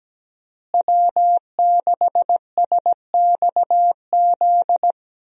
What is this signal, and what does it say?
Morse code, heard as a keyed tone, about 17 wpm, 700 Hz. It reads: W6SXZ
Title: USA Call Signs - 5 Characters - Repeated 3x - 17wpm